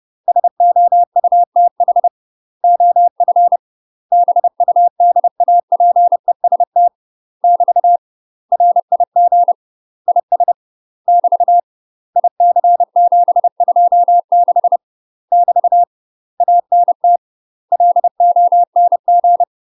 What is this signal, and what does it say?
Morse code, heard as a keyed tone, about 30 wpm, 700 Hz. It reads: SOUTH OF BUDAPEST <BT> RIG IS <BT> IC726 <BT> ANT LONG
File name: UK MASTER QSOs TRAINING / 30WPM / UK MASTER QSOs 0048